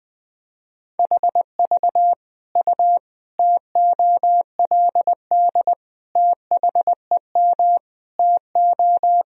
Morse code, heard as a keyed tone, about 20 wpm, 700 Hz. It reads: HV U TOLD THEM TO
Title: Sentences from Top 500 Words and Abbreviations 20wpm